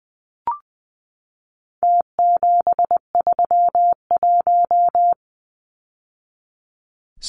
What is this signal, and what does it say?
Morse code, heard as a keyed tone, about 20 wpm, 700 Hz. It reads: T731
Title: Numbers - Contest Style - 4-Digit 20wpm